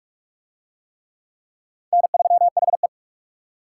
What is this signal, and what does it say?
Morse code, heard as a keyed tone, about 41 wpm, 700 Hz. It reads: N3HE